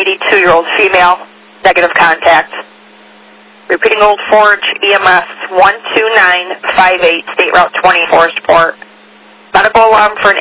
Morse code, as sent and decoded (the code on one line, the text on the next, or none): none